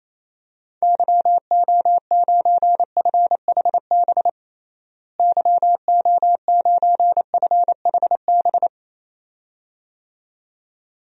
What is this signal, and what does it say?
Morse code, heard as a keyed tone, about 28 wpm, 700 Hz. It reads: YO9FHB YO9FHB
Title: DXCC-100 Call Signs - 6 Characters - Repeated 2x - 28wpm